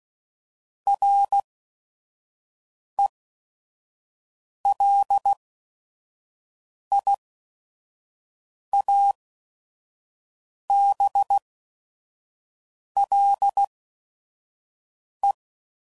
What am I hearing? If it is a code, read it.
Morse code, heard as a keyed tone, about 16 wpm, spaced as for 5 wpm, 800 Hz. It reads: RELIABLE
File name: CW Code Groups @5WPM / QSOs / WD0FBT 2